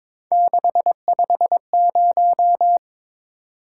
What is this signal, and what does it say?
Morse code, heard as a keyed tone, about 26 wpm, 700 Hz. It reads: 650
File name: Speed-Racing - Three Letter-Number Combinations - 22wpm